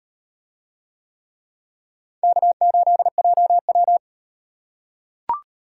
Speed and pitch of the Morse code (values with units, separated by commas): 38 wpm, 700 Hz